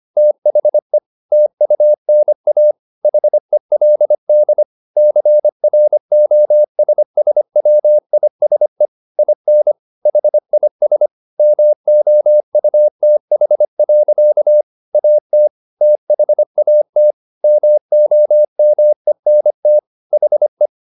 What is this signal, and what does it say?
Morse code, heard as a keyed tone, about 25 wpm, 600 Hz. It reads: THE TUNA HELD CROSSWISE IN HIS MOUTH. AT THAT MOMENT HE
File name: the old man and the sea part 1 25wpm